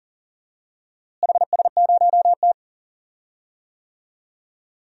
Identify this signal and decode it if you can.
Morse code, heard as a keyed tone, about 37 wpm, 700 Hz. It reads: HS0T
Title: DXCC-100 Call Signs - 4 Characters - Repeated 3x - 40wpm